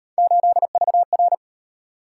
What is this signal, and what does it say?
Morse code, heard as a keyed tone, about 38 wpm, 700 Hz. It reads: 8VR